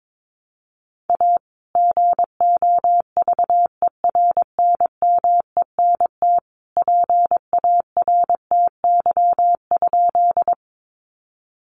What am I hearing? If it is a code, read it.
Morse code, heard as a keyed tone, about 22 wpm, 700 Hz. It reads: A GOVERNMENT PARTY?
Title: Sentences from Top 1400 Words 22wpm